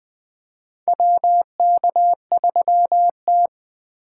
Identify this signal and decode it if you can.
Morse code, heard as a keyed tone, about 20 wpm, 700 Hz. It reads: WK3T
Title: USA Call Signs - 4 Characters - 3x Character Spacing - 20wpm